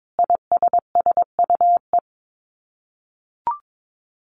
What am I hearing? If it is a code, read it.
Morse code, heard as a keyed tone, about 22 wpm, 700 Hz. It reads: ISSUE
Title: Sets of 5 Words Encore2 22wpm